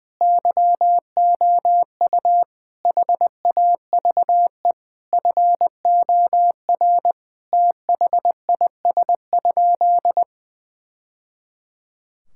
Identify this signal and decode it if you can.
Morse code, heard as a keyed tone, about 20 wpm, 700 Hz. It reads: YOU HAVE FOR THIS?
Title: Sentences from Top 1000 Words - Review Part 5 - 20wpm